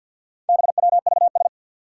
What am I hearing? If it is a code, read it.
Morse code, heard as a keyed tone, about 46 wpm, 700 Hz. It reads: BWVS
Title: Four Challenging Letter-Number Combinations 50wpm